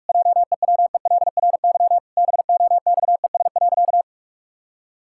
Morse code, heard as a keyed tone, about 46 wpm, 700 Hz. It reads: JEWELRY BOXES.
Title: Pangram Challenge v2 - 45wpm